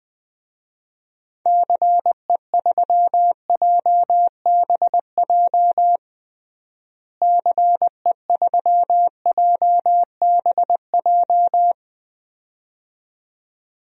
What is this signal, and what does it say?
Morse code, heard as a keyed tone, about 20 wpm, 700 Hz. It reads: CE3JBJ CE3JBJ
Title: DXCC-100 Call Signs - 6 Characters - Repeated 2x - 20wpm